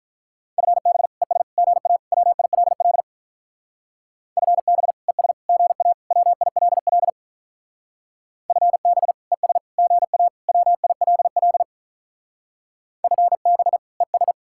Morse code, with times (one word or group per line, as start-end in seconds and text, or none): 0.6-1.1 FB
1.2-1.4 ES
1.6-2.0 GA
2.1-3.0 WILL
4.4-4.9 FB
5.1-5.3 ES
5.5-5.9 GA
6.1-7.1 WILL
8.5-9.1 FB
9.3-9.6 ES
9.8-10.3 GA
10.5-11.6 WILL
13.0-13.8 FB
14.0-14.3 ES